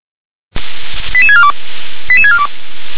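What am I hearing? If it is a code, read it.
Five-tone selective call (ZVEI), heard as quick runs of five tones, tones of about 70 ms. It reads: 88642 88642
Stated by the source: ZVEI five-tone selective call received on an SDR.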